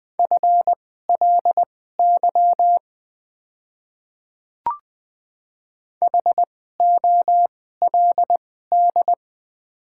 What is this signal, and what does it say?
Morse code, heard as a keyed tone, about 20 wpm, spaced as for 14 wpm, 690 Hz. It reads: FLY HOLD
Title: 2x Character Spacing - Top 500 Words - 20wpm